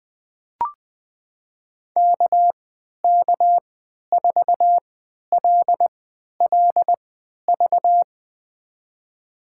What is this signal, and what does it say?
Morse code, heard as a keyed tone, about 20 wpm, spaced as for 11 wpm, 700 Hz. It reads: KK4LLV
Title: USA Call Signs - 6 Characters - 3x Character Spacing - 20wpm